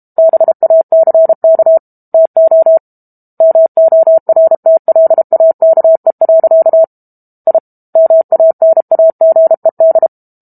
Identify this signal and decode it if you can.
Morse code, heard as a keyed tone, about 32 wpm, 650 Hz. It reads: BACK TO MORTLAKE. I MANAGED
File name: Book Two / 06-Chapter 6 - 32 WPM